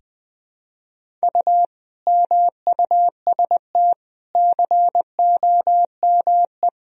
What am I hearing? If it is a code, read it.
Morse code, heard as a keyed tone, about 20 wpm, 700 Hz. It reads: U MUST COME
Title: Sentences from Top 200 Words and Abbreviations 20wpm